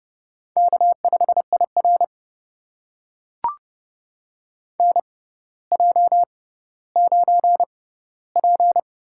Morse code, heard as a keyed tone, about 30 wpm, spaced as for 15 wpm, 700 Hz. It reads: K5IR NJ9P
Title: USA Call Signs - 6x Character Spacing - 30wpm